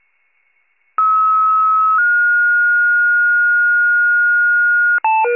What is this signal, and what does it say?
Two-tone page, heard as a one-second tone, then a three-second tone, about 1290 then 1430 Hz. It is heard from 1.0 to 5.0 s.